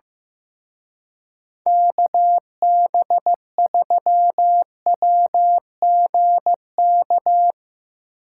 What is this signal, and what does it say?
Morse code, heard as a keyed tone, about 15 wpm, 700 Hz. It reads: KB3WGK
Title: Speed Racing - USA 2x3 Call Signs 15wpm